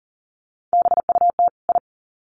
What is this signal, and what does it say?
Morse code, heard as a keyed tone, about 40 wpm, 700 Hz. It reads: BUT I
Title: Sets of 2 Words 40wpm